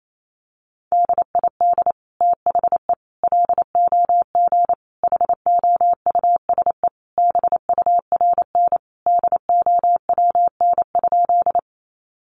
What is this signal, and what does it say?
Morse code, heard as a keyed tone, about 28 wpm, 700 Hz. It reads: DID THE LOG HOUSE BURN DOWN?
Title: Sentences from Top 1400 Words 28wpm